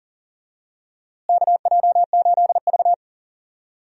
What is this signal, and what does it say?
Morse code, heard as a keyed tone, about 38 wpm, 700 Hz. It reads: KJ8V
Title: USA Call Signs - 4 Characters - Repeated 3x - 40wpm